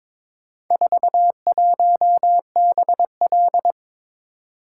Morse code, heard as a keyed tone, about 21 wpm, 700 Hz. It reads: 41BL
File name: practice / 22wpm / Four Letter-Number Combinations - 22wpm